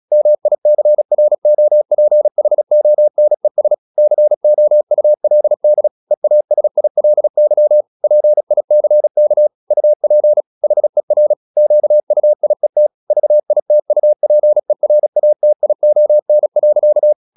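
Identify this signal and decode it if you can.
Morse code, heard as a keyed tone, about 36 wpm, 600 Hz. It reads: MICROPHONES COULD EASILY PICK UP HER QUIET VITUPERATION.
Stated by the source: Morse practice at 36 wpm, text of Ken Follett's novel Never, from CW med Gnister.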